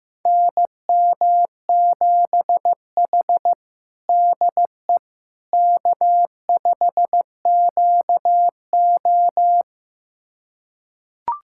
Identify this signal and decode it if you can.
Morse code, heard as a keyed tone, about 15 wpm, 700 Hz. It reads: NM7H DE K5QO
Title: USA Callsign DE Callsign - 4 Characters - 15wpm